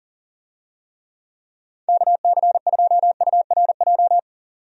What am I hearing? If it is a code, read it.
Morse code, heard as a keyed tone, about 40 wpm, 700 Hz. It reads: KC2URJ
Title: USA Call Signs - 6 Characters - 40wpm